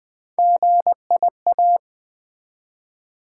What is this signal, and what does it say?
Morse code, heard as a keyed tone, about 21 wpm, 700 Hz. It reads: GIA
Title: Build A Word - 1 Letter At A Time - 5-Letter Words - 20wpm